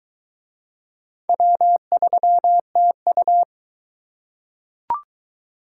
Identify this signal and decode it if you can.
Morse code, heard as a keyed tone, about 22 wpm, 700 Hz. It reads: W3TU